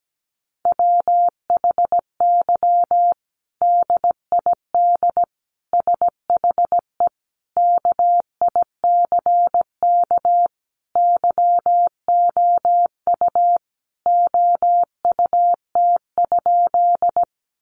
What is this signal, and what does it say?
Morse code, heard as a keyed tone, about 17 wpm, 700 Hz. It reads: WHY DID SHE KICK YOU OUT?
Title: Sentences from Top 1300 Words 17wpm